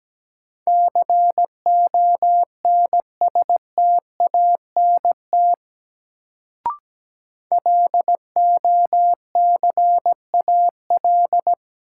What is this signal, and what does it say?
Morse code, heard as a keyed tone, about 17 wpm, 700 Hz. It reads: CONSTANT LOCAL
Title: Sentences from Top 1500 Words - New Word Drill Alt - 17wpm